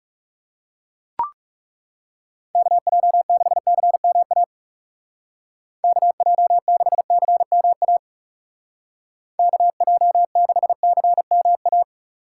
Morse code, heard as a keyed tone, about 38 wpm, 700 Hz. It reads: KJ6CMA KJ6CMA KJ6CMA